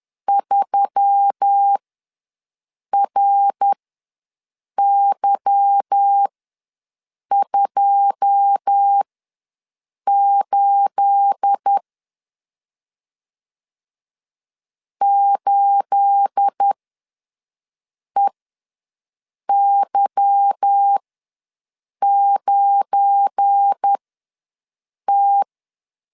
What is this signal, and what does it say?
Morse code, heard as a keyed tone, about 10 wpm, spaced as for 6 wpm, 800 Hz. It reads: 3RY28 8EY9T
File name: LEZIONI / 9 28 ws 10wpm 4124